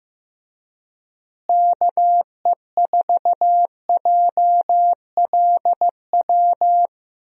Morse code, heard as a keyed tone, about 15 wpm, 700 Hz. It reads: KE4JLW
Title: USA Call Signs - 6 Characters - Repeated 2x - 15wpm